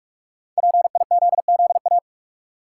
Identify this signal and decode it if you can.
Morse code, heard as a keyed tone, about 48 wpm, 700 Hz. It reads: PIZZA
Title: Five Letter Words - 45wpm